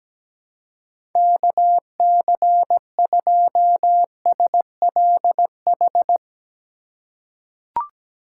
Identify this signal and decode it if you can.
Morse code, heard as a keyed tone, about 17 wpm, 700 Hz. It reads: KC2SLH
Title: USA Call Signs - 6 Characters - 3x Character Spacing - 17wpm